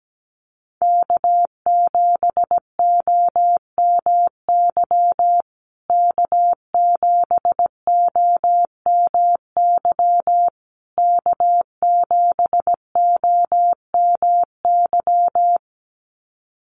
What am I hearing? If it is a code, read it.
Morse code, heard as a keyed tone, about 17 wpm, 700 Hz. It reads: K7OMY K7OMY K7OMY
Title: USA Call Signs Repeated 3x 17wpm